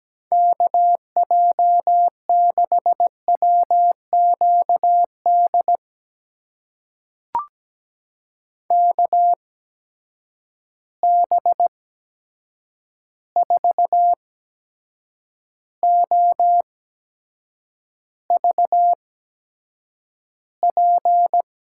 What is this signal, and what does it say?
Morse code, heard as a keyed tone, about 17 wpm, spaced as for 7 wpm, 700 Hz. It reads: KJ6WQD KB4OVP
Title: USA Call Signs - 6 Characters - 8x Character Spacing - 17wpm